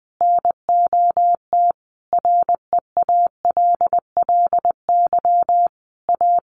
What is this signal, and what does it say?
Morse code, heard as a keyed tone, about 20 wpm, 700 Hz. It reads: NOT REALLY A
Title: Sentences from Top 1500 Words 20wpm